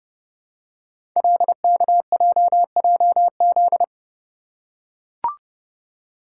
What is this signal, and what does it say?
Morse code, heard as a keyed tone, about 30 wpm, 700 Hz. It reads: LKJJZ
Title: Code Groups - 5 Letters - 30wpm